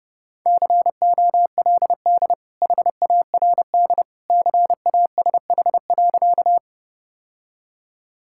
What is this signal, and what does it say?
Morse code, heard as a keyed tone, about 32 wpm, 700 Hz. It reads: COLD HARD CASH.
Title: Sentences from Top 1000 Words - Review Part 5 - 30wpm